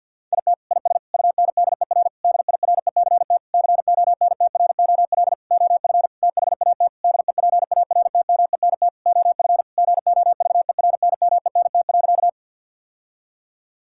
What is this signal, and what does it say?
Morse code, heard as a keyed tone, about 49 wpm, 700 Hz. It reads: IT IS UNDER DIRECT CONTROL OF THAT DEPARTMENT OF GOVERNMENT.
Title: Sentences from Top 1000 Words - Review Part 5 - 50wpm